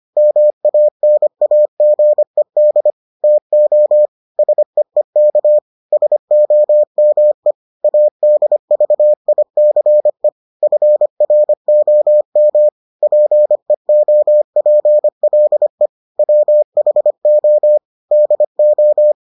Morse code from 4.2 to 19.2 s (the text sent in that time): SEEK SOME ADVICE FROM PEOPLE WHO DO